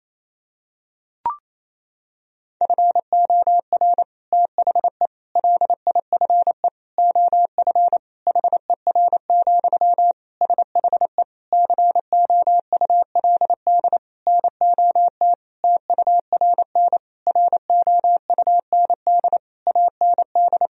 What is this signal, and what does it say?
Morse code, heard as a keyed tone, about 28 wpm, 700 Hz. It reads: FOR THE LIFE OF HER, SHE COULD NOT TURN ROUND AND
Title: Sentences from Top 1000 Words - Review Part 2 - 28wpm